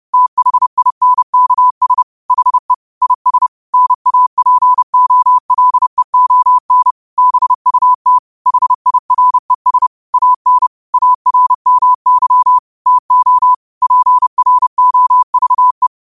Morse code, heard as a keyed tone, about 30 wpm, 1000 Hz. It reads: THINKS HE IS NAPOLEON BUT HIRES AN ARMY TO PROVE
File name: qotdfiles / 30wpm